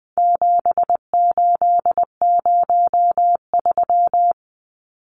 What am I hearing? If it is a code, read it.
Morse code, heard as a keyed tone, about 20 wpm, 700 Hz. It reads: 7803
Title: Numbers - 4-Digit 20wpm